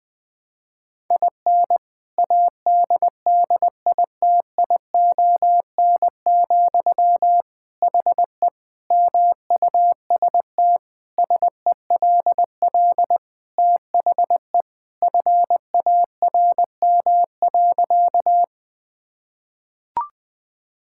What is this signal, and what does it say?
Morse code, heard as a keyed tone, about 20 wpm, 700 Hz. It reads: IN ADDITION, HE MUST SELL THE FARM.
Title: Sentences from Top 1000 Words - Review Part 6 - 20wpm